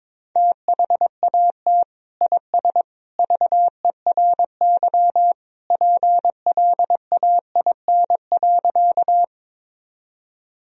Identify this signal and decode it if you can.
Morse code, heard as a keyed tone, about 22 wpm, 700 Hz. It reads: THAT IS VERY PLAIN.
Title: Sentences from Top 1000 Words - Review Part 2 - 22wpm